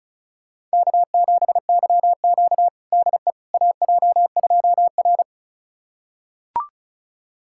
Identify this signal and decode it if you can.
Morse code, heard as a keyed tone, about 36 wpm, 700 Hz. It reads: K7YQ DE AJ2R
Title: USA Callsign DE Callsign - 4 Characters - 35wpm